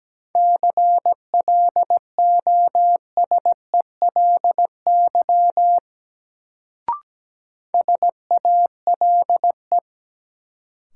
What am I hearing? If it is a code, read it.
Morse code, heard as a keyed tone, about 17 wpm, 700 Hz. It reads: CLOSELY SALE
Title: Sentences from Top 1400 Words - New Word Drill Alt - 17wpm